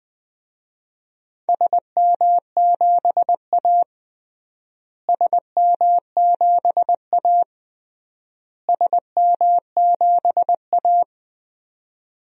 Morse code, heard as a keyed tone, about 19 wpm, 700 Hz. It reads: SM7A SM7A SM7A
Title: DXCC-100 Call Signs - 4 Characters - Repeated 3x - 20wpm